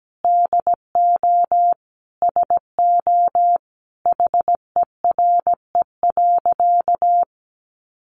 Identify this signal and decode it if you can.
Morse code, heard as a keyed tone, about 17 wpm, 700 Hz. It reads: DO SO HERE.
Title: Sentences from Top 600 Words 17wpm